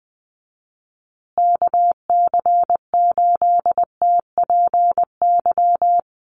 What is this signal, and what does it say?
Morse code, heard as a keyed tone, about 20 wpm, 700 Hz. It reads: KC8TPY
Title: USA Call Signs Repeated 3x 20wpm